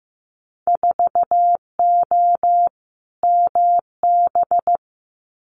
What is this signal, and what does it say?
Morse code, heard as a keyed tone, about 15 wpm, 700 Hz. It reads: 4O MB
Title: Winter Field Day Exchanges Encore 15wpm